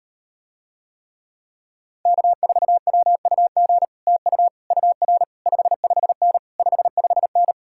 Morse code, heard as a keyed tone, about 39 wpm, 700 Hz. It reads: K4WUG TU UR 55N 55N